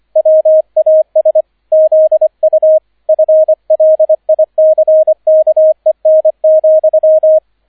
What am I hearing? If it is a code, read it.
Morse code, heard as a keyed tone, about 25 wpm, 630 Hz, with radio noise. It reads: WAS ZU FLICKEN,